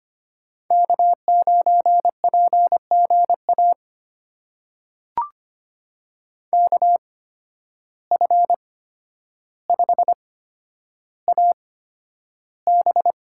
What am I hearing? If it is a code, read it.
Morse code, heard as a keyed tone, about 25 wpm, spaced as for 11 wpm, 700 Hz. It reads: K9PGA KF5AB
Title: USA Call Signs - 5 Characters - 8x Character Spacing - 25wpm